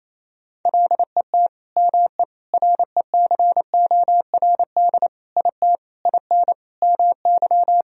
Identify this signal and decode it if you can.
Morse code, heard as a keyed tone, about 28 wpm, 700 Hz. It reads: LET ME RECORD IT IN MY
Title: Sentences from Top 1000 Words - Review Part 3 - 28wpm